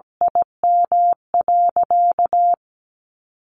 Morse code, heard as a keyed tone, about 17 wpm, 700 Hz. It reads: IM.
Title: Sentences from Top 1100 Words 17wpm